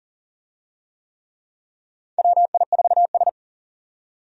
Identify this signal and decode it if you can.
Morse code, heard as a keyed tone, about 40 wpm, 700 Hz. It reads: WI4S
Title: USA Call Signs - 4 Characters - 40wpm